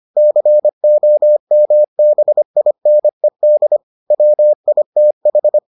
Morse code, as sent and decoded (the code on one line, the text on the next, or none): -.-. --- -- -... .. -. -.. / .-- .. - ....
COMBINED WITH